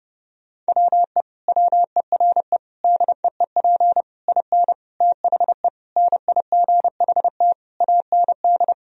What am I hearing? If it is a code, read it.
Morse code, heard as a keyed tone, about 30 wpm, 700 Hz. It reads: WE WERE DEEP IN THE NIGHT AND
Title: Sentences from Top 1000 Words - Review Part 3 - 30wpm